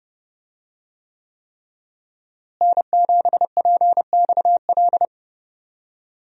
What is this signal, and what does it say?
Morse code, heard as a keyed tone, about 30 wpm, 700 Hz. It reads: N7PXL
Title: USA Call Signs - 5 Characters - Repeated 3x - 30wpm